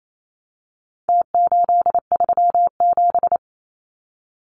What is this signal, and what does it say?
Morse code, heard as a keyed tone, about 28 wpm, 700 Hz. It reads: T837
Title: Numbers - Contest Style - 4-Digit 28wpm